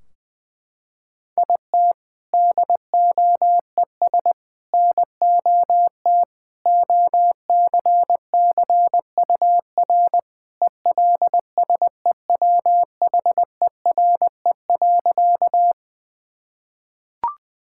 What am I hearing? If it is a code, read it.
Morse code, heard as a keyed tone, about 20 wpm, 700 Hz. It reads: IT DOES NOT OCCUR ELSEWHERE.
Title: Sentences from Top 1600 Words 20wpm